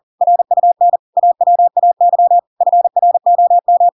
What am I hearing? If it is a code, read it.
Morse code, heard as a keyed tone, about 40 wpm, 700 Hz. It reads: RUN AWAY FROM
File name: Pinocchio / Pinocchio ch4 40 40 ik1hge